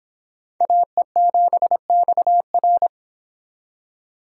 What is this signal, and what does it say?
Morse code, heard as a keyed tone, about 25 wpm, 700 Hz. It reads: AE7XR